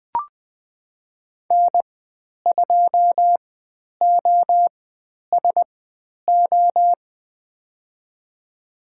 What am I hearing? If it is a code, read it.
Morse code, heard as a keyed tone, about 20 wpm, spaced as for 10 wpm, 700 Hz. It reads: N2OSO
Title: 095 Lesson 16g - Letter c - Call Signs 20wpm